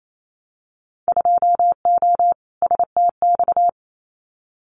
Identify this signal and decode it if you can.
Morse code, heard as a keyed tone, about 28 wpm, 700 Hz. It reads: 2O STX
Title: Winter Field Day Exchanges 28wpm